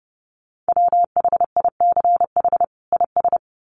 Morse code, heard as a keyed tone, about 30 wpm, 700 Hz. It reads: WHICH IS
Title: Sets of 2 Words 30wpm